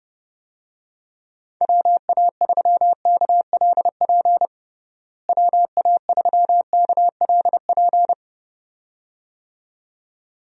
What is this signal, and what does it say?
Morse code, heard as a keyed tone, about 30 wpm, 700 Hz. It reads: WA3KLP WA3KLP
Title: USA Call Signs - 6 Characters - Repeated 2x - 30wpm